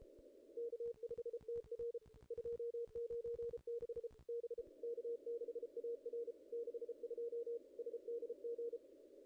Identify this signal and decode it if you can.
Morse code, heard as a keyed tone, about 34 wpm, 470 Hz. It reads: K5TR 296 B K6AR 62 SDG
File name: contests / ss-cw-k5tr-op-n5rz -0000